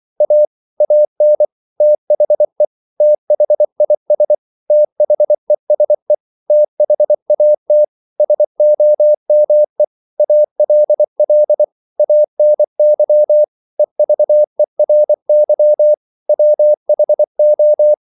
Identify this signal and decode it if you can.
Morse code, heard as a keyed tone, about 24 wpm, 600 Hz. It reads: A AN THE THIS THESE THAT SOME ALL ANY EVERY WHO
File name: CW / Practice / 500MostCommonEnglishWords24